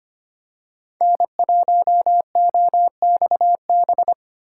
Morse code, heard as a keyed tone, about 25 wpm, 700 Hz. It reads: N1OXB
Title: USA Call Signs - 5 Characters - 2x Character Spacing - 25wpm